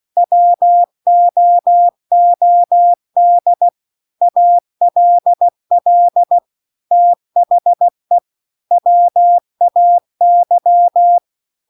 Morse code, heard as a keyed tone, about 16 wpm, 700 Hz. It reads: WOOD ALL THE WAY
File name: CW-The-Big-Twoheaded-River-2-16WPM